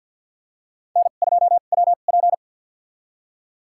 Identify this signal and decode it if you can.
Morse code, heard as a keyed tone, about 52 wpm, spaced as for 37 wpm, 700 Hz. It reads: N2WP
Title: USA Call Signs - 4 Characters - 2x Character Spacing - 50wpm